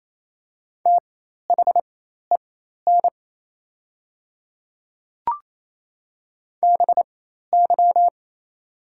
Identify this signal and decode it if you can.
Morse code, heard as a keyed tone, about 28 wpm, spaced as for 13 wpm, 700 Hz. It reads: THEN BY
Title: 4x Character Spacing - Top 200 Words - 28wpm